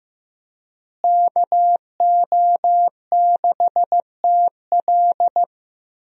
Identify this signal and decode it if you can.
Morse code, heard as a keyed tone, about 15 wpm, 700 Hz. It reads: KO6TL
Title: USA Call Signs - 5 Characters - Repeated 2x - 15wpm